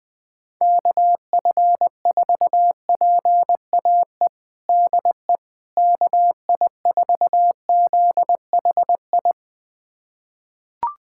Morse code, heard as a keyed tone, about 20 wpm, 700 Hz. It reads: KF4PAE DE KI4ZHI
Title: USA Callsign DE Callsign - 6 Characters - 20wpm